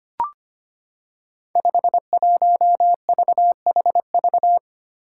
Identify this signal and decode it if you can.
Morse code, heard as a keyed tone, about 25 wpm, 700 Hz. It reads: 51VHV
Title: Five Challenging Letter-Number Combinations 25wpm